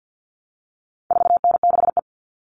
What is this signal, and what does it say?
Morse code, heard as a keyed tone, about 50 wpm, 700 Hz. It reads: 4N6E
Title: All Cut Numbers - 4-Digit 50wpm